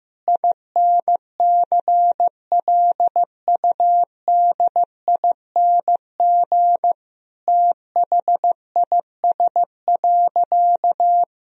Morse code, heard as a keyed tone, about 15 wpm, 700 Hz. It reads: INCLUDING THIS.
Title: Sentences from Top 1000 Words - Review Part 4 - 15wpm